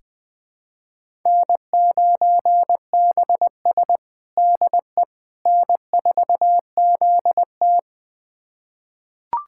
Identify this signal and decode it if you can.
Morse code, heard as a keyed tone, about 20 wpm, 700 Hz. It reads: N9BS DE N4ZT
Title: USA Callsign DE Callsign - 4 Characters - 20wpm